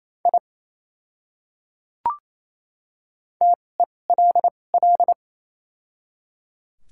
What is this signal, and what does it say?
Morse code, heard as a keyed tone, about 28 wpm, spaced as for 20 wpm, 700 Hz. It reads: I TELL
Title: 2x Character Spacing - Top 200 Words - 28wpm